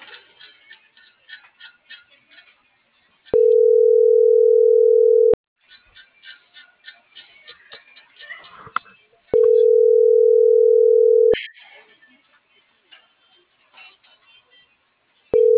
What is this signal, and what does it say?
Ringback tone (North America), 440 + 480 Hz.